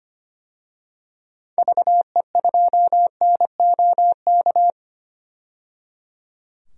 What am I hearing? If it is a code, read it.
Morse code, heard as a keyed tone, about 25 wpm, 700 Hz. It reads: VE2NOK
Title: DXCC-100 Call Signs - 6 Characters - 25wpm